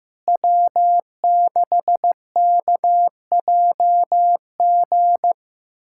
Morse code, heard as a keyed tone, about 15 wpm, 700 Hz. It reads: W6KJG
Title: USA Callsign DE Callsign - 5 Characters - 15wpm